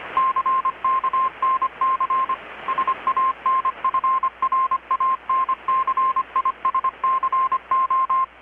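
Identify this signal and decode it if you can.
Morse code, heard as a keyed tone, about 25 wpm, 1050 Hz. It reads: CKNC SANFRANCISCO
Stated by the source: Morse from KPH's Night of Nights XX maritime radiotelegraphy event, received in Chiapas, Mexico about 3,800 km away on a Kenwood R-600 with a grounded 10 m long-wire antenna.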